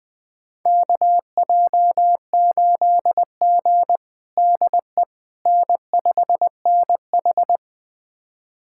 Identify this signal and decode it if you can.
Morse code, heard as a keyed tone, about 20 wpm, 700 Hz. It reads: KJ8G DE N5NH
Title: USA Callsign DE Callsign - 4 Characters - 20wpm